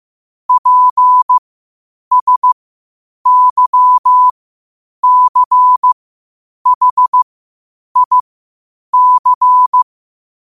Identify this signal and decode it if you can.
Morse code, heard as a keyed tone, about 15 wpm, spaced as for 8 wpm, 1000 Hz. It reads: PSYCHIC
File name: qotdfiles / 8wpm